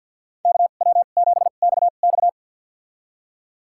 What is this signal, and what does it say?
Morse code, heard as a keyed tone, about 48 wpm, spaced as for 35 wpm, 700 Hz. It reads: KW7XX
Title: USA Call Signs - 5 Characters - 2x Character Spacing - 50wpm